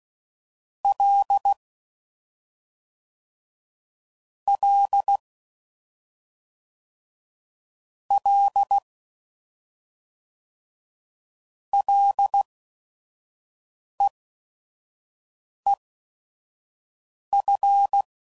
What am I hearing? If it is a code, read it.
Morse code, heard as a keyed tone, about 16 wpm, spaced as for 5 wpm, 800 Hz. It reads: L L L LEEF
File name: Mixed / Short Runs / SinglePrefixes